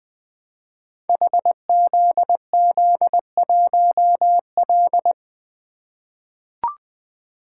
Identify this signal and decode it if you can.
Morse code, heard as a keyed tone, about 20 wpm, 700 Hz. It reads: HZZ1L
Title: Code Groups - 5 Letter:Numbers - 20wpm